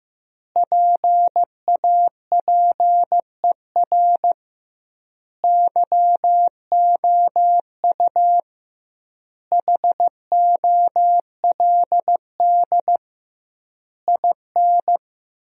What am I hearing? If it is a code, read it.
Morse code, heard as a keyed tone, about 16 wpm, 700 Hz. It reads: PAPER YOU HOLD IN
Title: 2x Word-Spacing - Sentences from Top 400 Words - 15wpm